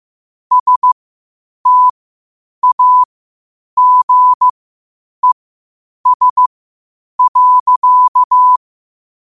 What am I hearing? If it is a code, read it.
Morse code, heard as a keyed tone, about 15 wpm, spaced as for 8 wpm, 1000 Hz. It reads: STAGES.
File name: qotdfiles / 8wpm